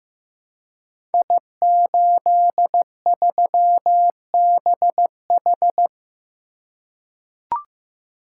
Morse code, heard as a keyed tone, about 15 wpm, 700 Hz. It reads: I83BH
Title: Code Groups - 5 Letter:Numbers - 15wpm